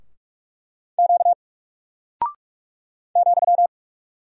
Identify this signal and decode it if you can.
Morse code, heard as a keyed tone, about 44 wpm, 700 Hz. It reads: Q ,
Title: Single Character (Letter:Number:Punctuation:Prosign) Alt 45wpm